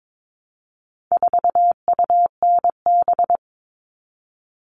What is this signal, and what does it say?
Morse code, heard as a keyed tone, about 22 wpm, 700 Hz. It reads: 4UNB
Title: All Cut Numbers - 4-Digit 22wpm